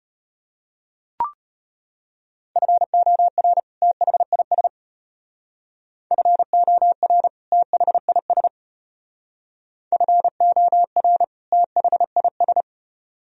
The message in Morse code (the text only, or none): FOR THIS FOR THIS FOR THIS